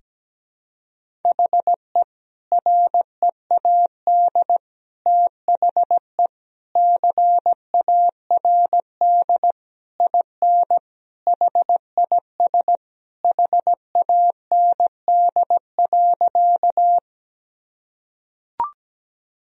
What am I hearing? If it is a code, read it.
Morse code, heard as a keyed tone, about 17 wpm, 700 Hz. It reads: HE READ THE CARD IN HIS HAND.
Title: Sentences from Top 1000 Words - Review Part 4 - 17wpm